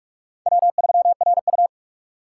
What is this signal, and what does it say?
Morse code, heard as a keyed tone, about 46 wpm, 700 Hz. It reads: W3RU
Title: USA Call Signs - 4x Character Spacing - 45wpm